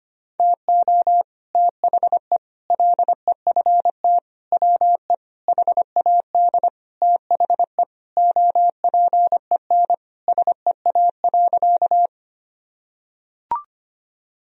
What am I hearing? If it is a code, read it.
Morse code, heard as a keyed tone, about 25 wpm, 700 Hz. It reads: TO THE LEFT WE HAD THE OPEN SEA.
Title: Sentences from Top 1000 Words - Review Part 1 - 25wpm